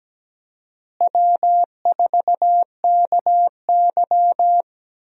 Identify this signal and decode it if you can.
Morse code, heard as a keyed tone, about 17 wpm, 700 Hz. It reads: W4KY
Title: USA Call Signs - 4 Characters - Repeated 2x - 17wpm